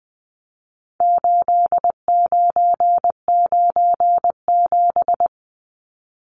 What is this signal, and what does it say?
Morse code, heard as a keyed tone, about 20 wpm, 700 Hz. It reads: 8997
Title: Numbers - 4-Digit 20wpm